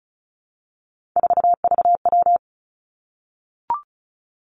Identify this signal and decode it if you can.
Morse code, heard as a keyed tone, about 36 wpm, 700 Hz. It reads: 4VW
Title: Three Challenging Letter-Number Combinations 35wpm